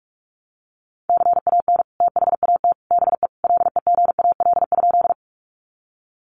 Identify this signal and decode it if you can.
Morse code, heard as a keyed tone, about 44 wpm, 700 Hz. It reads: CAN THAT BE LEGAL?
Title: Sentences from Top 1200 Words 45wpm